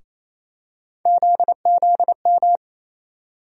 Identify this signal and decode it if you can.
Morse code, heard as a keyed tone, about 28 wpm, 700 Hz. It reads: ZZM
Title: USA Call Sign Postfixes - 3 Letters 28wpm